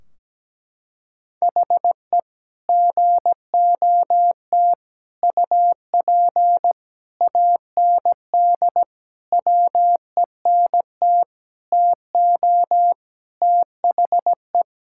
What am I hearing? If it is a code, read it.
Morse code, heard as a keyed tone, about 17 wpm, 700 Hz. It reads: HE GOT UP AND WENT TO THE
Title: Sentences from Top 1600 Words 17wpm